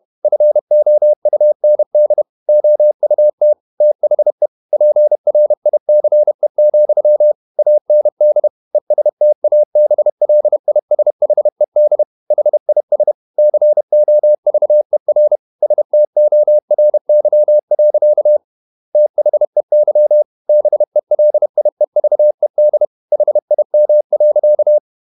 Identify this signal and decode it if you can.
Morse code, heard as a keyed tone, about 31 wpm, 600 Hz. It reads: FOUND OUT THE PRICE, AND ESTABLISHED HIS COVER STORY. THEY BELIEVED HIM.
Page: Never 31wpm | CW med Gnister